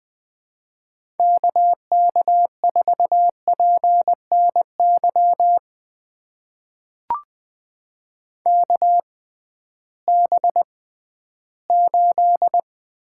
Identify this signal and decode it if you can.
Morse code, heard as a keyed tone, about 20 wpm, spaced as for 13 wpm, 700 Hz. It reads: KK4PNY KB8
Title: USA Call Signs - 6 Characters - 6x Character Spacing - 20wpm